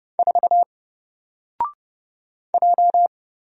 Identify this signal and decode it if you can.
Morse code, heard as a keyed tone, about 31 wpm, 700 Hz. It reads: 4 J
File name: Single-Letter-Number-30wpm